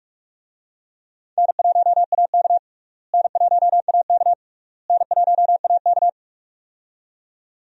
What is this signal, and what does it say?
Morse code, heard as a keyed tone, about 46 wpm, 700 Hz. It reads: N1AK N1AK N1AK
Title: USA Call Signs - 4 Characters - Repeated 3x - 45wpm